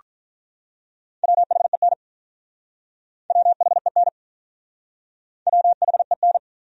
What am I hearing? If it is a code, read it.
Morse code, heard as a keyed tone, about 45 wpm, 700 Hz. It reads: WHEN WHEN WHEN